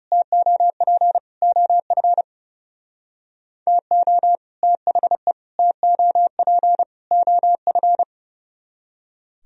Morse code, heard as a keyed tone, about 38 wpm, 700 Hz. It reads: TOP OF TO THE TOP OF